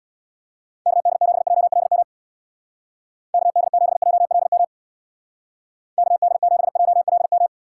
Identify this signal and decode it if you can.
Morse code, heard as a keyed tone, about 63 wpm, 700 Hz. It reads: KD7JLM KD7JLM KD7JLM